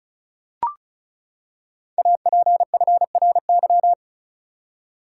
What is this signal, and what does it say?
Morse code, heard as a keyed tone, about 36 wpm, 700 Hz. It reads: APFRY
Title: Code Groups - 5 Letters - 35wpm